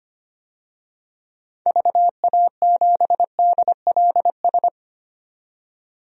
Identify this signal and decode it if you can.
Morse code, heard as a keyed tone, about 25 wpm, 700 Hz. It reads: VA7DLS
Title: DXCC-100 Call Signs - 6 Characters - 25wpm